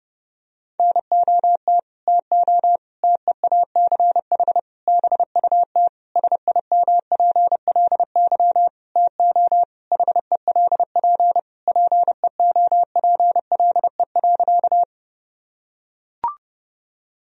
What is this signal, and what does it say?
Morse code, heard as a keyed tone, about 30 wpm, 700 Hz. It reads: NOT TO TEACH BUT SIMPLY TO HELP PEOPLE.
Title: Sentences from Top 1000 Words - Review Part 4 - 30wpm